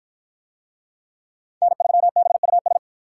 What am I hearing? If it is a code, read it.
Morse code, heard as a keyed tone, about 53 wpm, 700 Hz. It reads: N3BUS